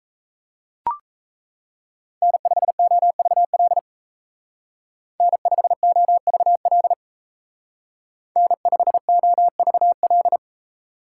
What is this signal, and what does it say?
Morse code, heard as a keyed tone, about 33 wpm, 700 Hz. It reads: N5OVL N5OVL N5OVL
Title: Speed Racing - USA Call Signs 28wpm